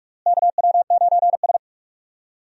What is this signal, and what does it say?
Morse code, heard as a keyed tone, about 43 wpm, 700 Hz. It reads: KW9S
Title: USA Call Signs - 4 Characters - Repeated 2x - 45wpm